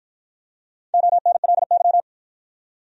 Morse code, heard as a keyed tone, about 53 wpm, 700 Hz. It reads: ONLY